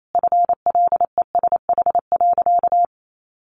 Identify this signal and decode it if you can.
Morse code, heard as a keyed tone, about 28 wpm, 700 Hz. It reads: FLESH.
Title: A Sound Of Thunder 28wpm